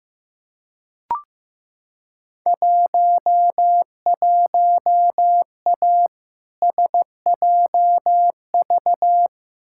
Morse code, heard as a keyed tone, about 15 wpm, 700 Hz. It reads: 11A SJV
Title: Field Day Exchanges Encore - 15wpm